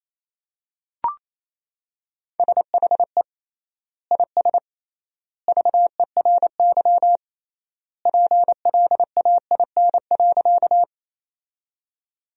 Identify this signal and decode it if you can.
Morse code, heard as a keyed tone, about 28 wpm, 700 Hz. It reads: SHE IS VERY PLAIN.
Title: 3x Word-Spacing - Sentences from Top 400 Words - 28wpm